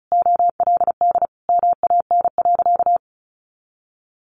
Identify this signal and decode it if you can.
Morse code, heard as a keyed tone, about 34 wpm, 700 Hz. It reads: OLD MAN.
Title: Sentences from Top 500 Words 35wpm